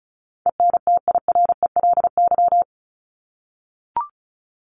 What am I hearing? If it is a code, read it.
Morse code, heard as a keyed tone, about 34 wpm, 700 Hz. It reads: ENTIRELY
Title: LY Suffix - Top 100 Words with Base Words 35wpm